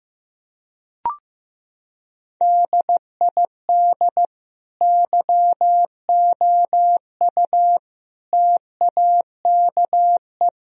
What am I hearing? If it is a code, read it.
Morse code, heard as a keyed tone, about 15 wpm, 700 Hz. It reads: DID YOU TAKE
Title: Sentences from Top 1000 Words - Review Part 1 - 15wpm